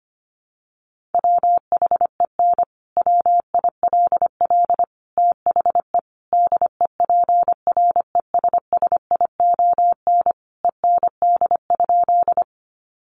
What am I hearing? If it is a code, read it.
Morse code, heard as a keyed tone, about 25 wpm, 700 Hz. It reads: WHEN WILL THE DEPRESSION END?
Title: Sentences from Top 1300 Words 25wpm